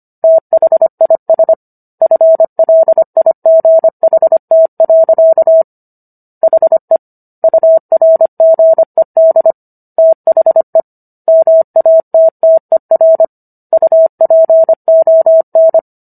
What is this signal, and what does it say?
Morse code, heard as a keyed tone, about 25 wpm, 650 Hz. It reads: THIS FLIGHT. HE URGED THE MATTER UPON
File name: Book One / 16-Chapter 16 - 25 WPM